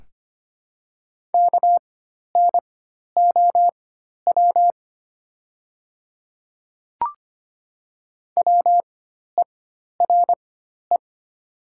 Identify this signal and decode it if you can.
Morse code, heard as a keyed tone, about 25 wpm, spaced as for 12 wpm, 700 Hz. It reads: KNOW WERE
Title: 4x Character Spacing - Top 500 Words - 25wpm